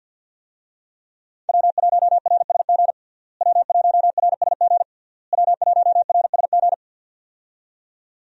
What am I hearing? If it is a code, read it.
Morse code, heard as a keyed tone, about 49 wpm, 700 Hz. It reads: W1RSG W1RSG W1RSG
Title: USA Call Signs - 5 Characters - Repeated 3x - 50wpm